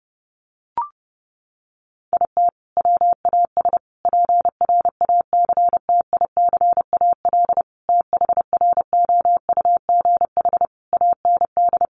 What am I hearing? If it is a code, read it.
Morse code, heard as a keyed tone, about 30 wpm, 700 Hz. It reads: IT WAS PRACTICAL THROUGH AND
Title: Sentences from Top 1400 Words 30wpm